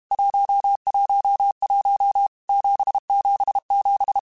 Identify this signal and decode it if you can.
Morse code, heard as a keyed tone, about 32 wpm, 800 Hz. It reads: 111 777
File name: Basic Letter Groups By7Groups / JZ17 / JZ17@30